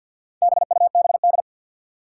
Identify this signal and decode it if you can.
Morse code, heard as a keyed tone, about 52 wpm, 700 Hz. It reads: BUBD
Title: Four Challenging Letter-Number Combinations 50wpm